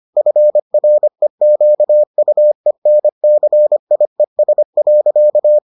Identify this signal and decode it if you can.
Morse code, heard as a keyed tone, about 25 wpm, 600 Hz. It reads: FREQUENCIES.